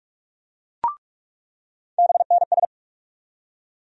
Morse code, heard as a keyed tone, about 46 wpm, 700 Hz. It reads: BNS
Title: USA Call Sign Postfixes - 3 Letters 45wpm